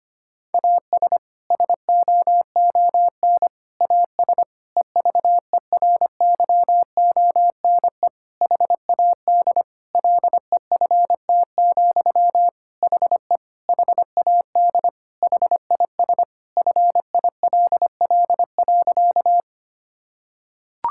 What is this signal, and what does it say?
Morse code, heard as a keyed tone, about 25 wpm, 700 Hz. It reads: AS SOON AS EVERYONE HAD LEFT, HE HAD HIS FILL.
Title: Sentences from Top 1000 Words - Review Part 5 - 25wpm